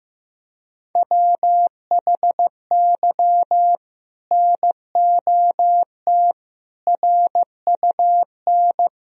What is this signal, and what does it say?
Morse code, heard as a keyed tone, about 15 wpm, 700 Hz. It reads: WHY NOT RUN
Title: Sentences from Top 1000 Words - Review Part 1 - 15wpm